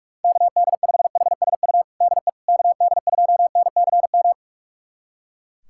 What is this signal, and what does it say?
Morse code, heard as a keyed tone, about 44 wpm, 700 Hz. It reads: KD5HSU DE KD2NCM
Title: USA Callsign DE Callsign - 6 Characters - 45wpm